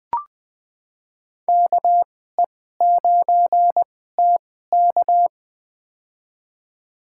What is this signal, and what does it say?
Morse code, heard as a keyed tone, about 21 wpm, spaced as for 15 wpm, 700 Hz. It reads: KE9TK
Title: USA Call Signs - 5 Characters - 2x Character Spacing - 20wpm